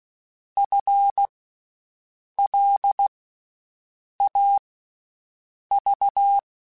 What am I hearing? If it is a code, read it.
Morse code, heard as a keyed tone, about 16 wpm, spaced as for 6 wpm, 800 Hz. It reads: FLAV
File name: Special INGs-mp3s / Special INGs-mp3s / Slow ing @15